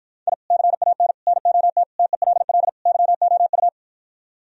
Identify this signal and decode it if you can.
Morse code, heard as a keyed tone, about 52 wpm, 700 Hz. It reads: I CAN NOT TELL YOU